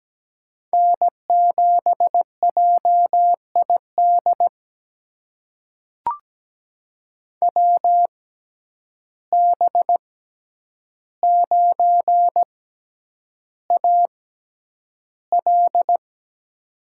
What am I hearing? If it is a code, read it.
Morse code, heard as a keyed tone, about 17 wpm, spaced as for 9 wpm, 700 Hz. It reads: N7JID WB9AL
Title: USA Call Signs - 6x Character Spacing - 17wpm